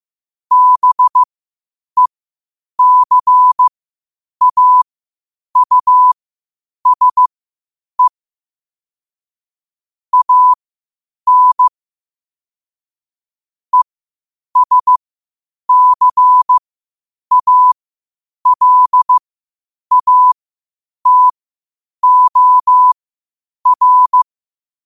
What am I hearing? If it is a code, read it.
Morse code, heard as a keyed tone, about 15 wpm, spaced as for 8 wpm, 1000 Hz. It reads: BECAUSE AN ESCALATOR
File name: qotdfiles / 8wpm